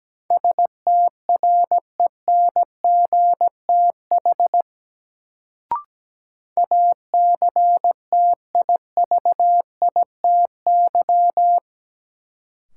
Morse code, heard as a keyed tone, about 17 wpm, 700 Hz. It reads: STRENGTH ACTIVITY
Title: Sentences from Top 1000 Words - Review Part 4 - New Word Drill - 17wpm